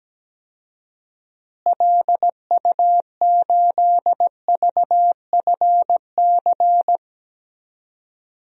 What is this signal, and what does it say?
Morse code, heard as a keyed tone, about 17 wpm, 700 Hz. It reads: LU8VFC
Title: DXCC-100 Call Signs - 6 Characters - Repeated 3x - 17wpm